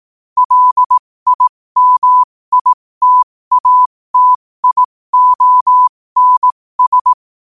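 Morse code, heard as a keyed tone, about 18 wpm, spaced as for 15 wpm, 1000 Hz. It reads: LIMITATIONS
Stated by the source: Morse quote-of-the-day practice at 15 WPM from Morse Resource.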